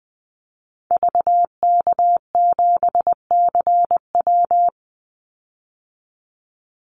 Morse code, heard as a keyed tone, about 20 wpm, 700 Hz. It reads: VK7CW
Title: Call Signs Repeated 2x - 20wpm